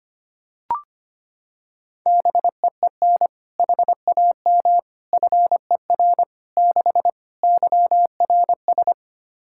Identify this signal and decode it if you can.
Morse code, heard as a keyed tone, about 25 wpm, 700 Hz. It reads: BEEN HAM FER 6 YRS
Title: Practice QSO - Exchange 3 - More Info - 25wpm